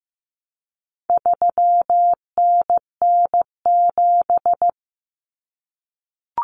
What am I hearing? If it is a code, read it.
Morse code, heard as a keyed tone, about 15 wpm, 700 Hz. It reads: 3NN7
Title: Common Cut Numbers - 4-Digit 15wpm